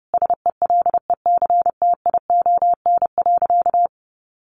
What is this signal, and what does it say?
Morse code, heard as a keyed tone, about 30 wpm, 700 Hz. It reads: SELECTION.
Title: Sentences from Top 1400 Words 30wpm